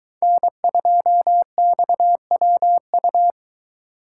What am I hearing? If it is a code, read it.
Morse code, heard as a keyed tone, about 22 wpm, 700 Hz. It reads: N2XWU